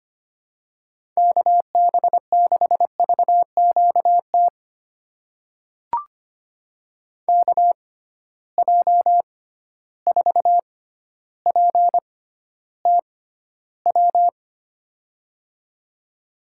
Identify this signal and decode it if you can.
Morse code, heard as a keyed tone, about 25 wpm, spaced as for 13 wpm, 700 Hz. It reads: KB6VQT KJ4PTW
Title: USA Call Signs - 6 Characters - 6x Character Spacing - 25wpm